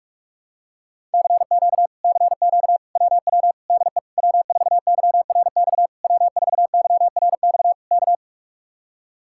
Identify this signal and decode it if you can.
Morse code, heard as a keyed tone, about 44 wpm, 700 Hz. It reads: CQ CQ WW DE W4YRX W4YRX K